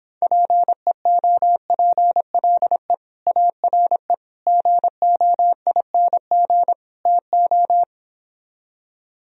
Morse code, heard as a keyed tone, about 26 wpm, 700 Hz. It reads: PEOPLE ARE GOING TO